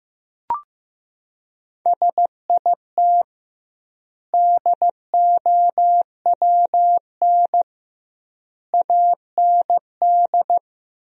Morse code, heard as a keyed tone, about 15 wpm, 700 Hz. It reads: SIT DOWN AND
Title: 2x Word-Spacing - Sentences from Top 500 Words - 15wpm